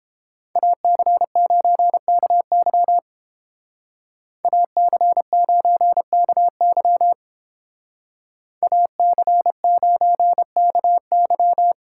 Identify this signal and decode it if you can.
Morse code, heard as a keyed tone, about 28 wpm, 700 Hz. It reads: AC9KY AC9KY AC9KY